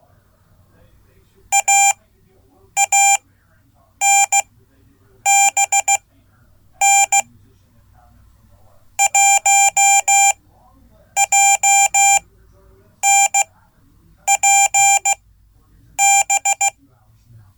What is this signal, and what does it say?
Morse code, heard as a keyed tone, about 15 wpm, spaced as for 8 wpm, 790 Hz. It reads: AANBN 1JNPB
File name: Code / 5min-b